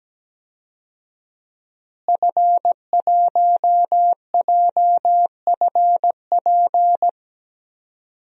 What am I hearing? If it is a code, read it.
Morse code, heard as a keyed tone, about 17 wpm, 700 Hz. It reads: F1JFP
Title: DXCC-100 Call Signs - 5 Characters - Repeated 2x - 17wpm